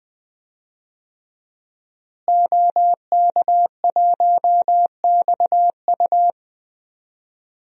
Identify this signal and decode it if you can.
Morse code, heard as a keyed tone, about 20 wpm, 700 Hz. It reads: OK1XU
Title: DXCC-100 Call Signs - 5 Characters - Repeated 2x - 20wpm